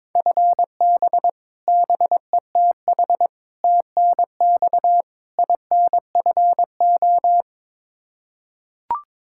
Morse code, heard as a keyed tone, about 22 wpm, 700 Hz. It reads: FB BETH TNX INFO
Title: Practice QSO - Exchange 3 - More Info - 22wpm